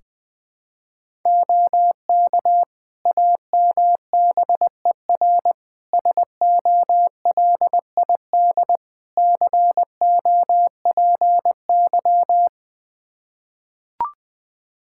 Morse code, heard as a keyed tone, about 20 wpm, 700 Hz. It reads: OK AMBER SOLID COPY
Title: Practice QSO - Exchange 3 - More Info - 20wpm